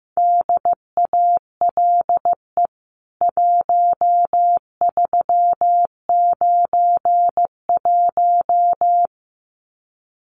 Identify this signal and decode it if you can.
Morse code, heard as a keyed tone, about 14 wpm, 700 Hz. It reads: DALE 1391
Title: CWT Exchange 15wpm